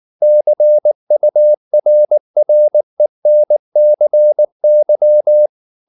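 Morse code, heard as a keyed tone, about 19 wpm, 600 Hz. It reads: CURRENCY
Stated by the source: Morse code at 19 wpm from CW med Gnister, reading Ken Follett's novel Never in English.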